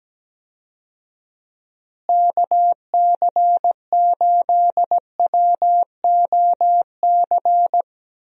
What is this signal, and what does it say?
Morse code, heard as a keyed tone, about 17 wpm, 700 Hz. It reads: KC8WOC
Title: USA Call Signs - 6 Characters - Repeated 2x - 17wpm